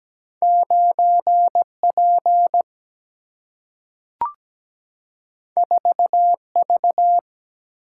Morse code, heard as a keyed tone, about 17 wpm, 700 Hz. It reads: 9 P 4 V
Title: Single-Letter Single-Number 17wpm